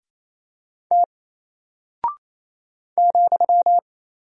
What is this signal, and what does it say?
Morse code, heard as a keyed tone, about 28 wpm, 700 Hz. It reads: T ,
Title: Single Character (Letter:Number:Punctuation:Prosign) Alt 28wpm